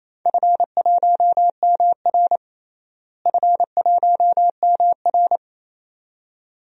Morse code, heard as a keyed tone, about 28 wpm, 700 Hz. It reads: F1MR F1MR
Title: DXCC-100 Call Signs - 4 Characters - Repeated 2x - 28wpm